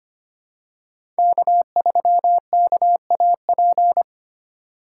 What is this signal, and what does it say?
Morse code, heard as a keyed tone, about 25 wpm, 700 Hz. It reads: K3KAP
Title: USA Call Signs - 5 Characters - 2x Character Spacing - 25wpm